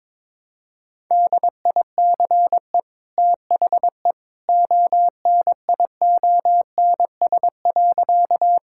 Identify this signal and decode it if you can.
Morse code, heard as a keyed tone, about 22 wpm, 700 Hz. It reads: DICE THE ONIONS.
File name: practice / 22wpm / Easy Sentences with Hints - 22wpm